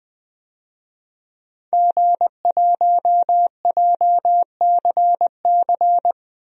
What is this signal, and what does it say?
Morse code, heard as a keyed tone, about 20 wpm, 700 Hz. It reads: G1JCC
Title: DXCC-100 Call Signs - 5 Characters - 20wpm